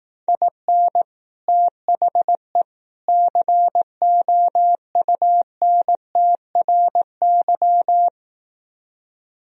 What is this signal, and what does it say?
Morse code, heard as a keyed tone, about 18 wpm, 700 Hz. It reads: IN THE COUNTRY
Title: Speed-Racing - Sets of 3 Words 15wpm